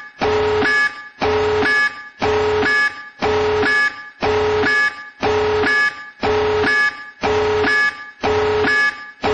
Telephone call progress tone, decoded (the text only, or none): busy tone